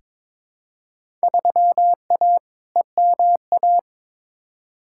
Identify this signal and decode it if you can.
Morse code, heard as a keyed tone, about 22 wpm, 700 Hz. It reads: 3A EMA
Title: Field Day Exchanges - 22wpm